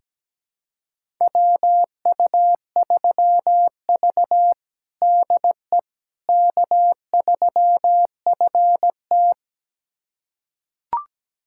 Morse code, heard as a keyed tone, about 17 wpm, 700 Hz. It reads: WU3V DE K3FT
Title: USA Callsign DE Callsign - 4 Characters - 17wpm